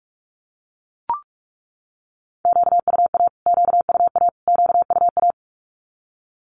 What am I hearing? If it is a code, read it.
Morse code, heard as a keyed tone, about 44 wpm, 700 Hz. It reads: QUA QUA QUA
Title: Q-Codes Repeated 3x 45wpm